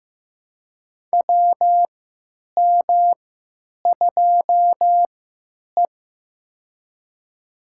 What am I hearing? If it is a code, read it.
Morse code, heard as a keyed tone, about 15 wpm, spaced as for 9 wpm, 700 Hz. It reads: WM2E
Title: USA Call Signs - 3x Character Spacing - 15wpm